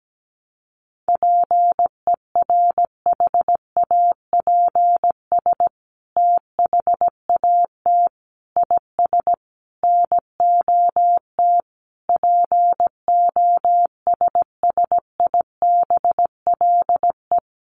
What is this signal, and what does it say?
Morse code, heard as a keyed tone, about 17 wpm, 700 Hz. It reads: PERHAPS THAT IS NOT POSSIBLE
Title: Sentences from Top 900 Words 17wpm